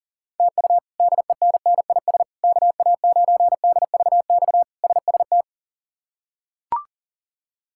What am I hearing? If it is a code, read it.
Morse code, heard as a keyed tone, about 40 wpm, 700 Hz. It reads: TU DENNIS KA9DVX SST
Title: K1USN SST QSO Practice - 40wpm